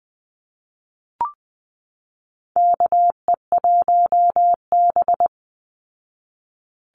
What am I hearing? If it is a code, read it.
Morse code, heard as a keyed tone, about 20 wpm, 700 Hz. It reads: KE1B
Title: Call Signs Repeated 2x - 20wpm